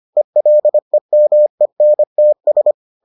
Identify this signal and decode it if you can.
Morse code, heard as a keyed tone, about 25 wpm, 600 Hz. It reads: ELEMENTS